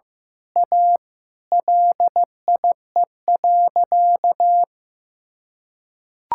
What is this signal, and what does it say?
Morse code, heard as a keyed tone, about 16 wpm, 700 Hz. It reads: A LIE.
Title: Sentences from Top 1600 Words 15wpm